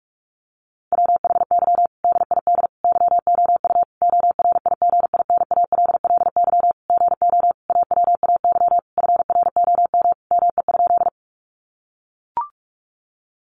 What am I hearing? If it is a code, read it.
Morse code, heard as a keyed tone, about 44 wpm, 700 Hz. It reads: WHY DID YOU ORIGINALLY GO AWAY FROM ME?
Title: Sentences from Top 1500 Words 45wpm